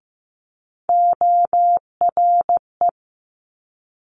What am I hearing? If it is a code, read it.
Morse code, heard as a keyed tone, about 15 wpm, 700 Hz. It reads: ORE
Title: Multi-letter Phonemes Quick Fire 15wpm